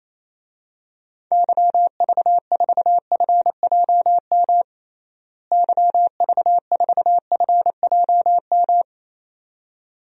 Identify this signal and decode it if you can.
Morse code, heard as a keyed tone, about 28 wpm, 700 Hz. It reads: YV4FJM YV4FJM
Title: DXCC-100 Call Signs - 6 Characters - Repeated 2x - 28wpm